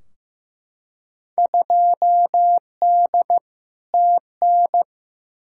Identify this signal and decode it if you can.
Morse code, heard as a keyed tone, about 15 wpm, 700 Hz. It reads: 2D TN
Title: Field Day Exchanges Encore - 15wpm